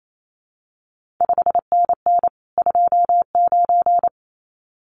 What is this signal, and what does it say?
Morse code, heard as a keyed tone, about 28 wpm, 700 Hz. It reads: 5NN 29
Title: CQ WW DX Contest Exchange - 28wpm